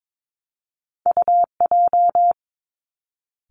Morse code, heard as a keyed tone, about 22 wpm, 700 Hz. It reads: U J
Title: Two Challenging Letter-Number Combinations 22wpm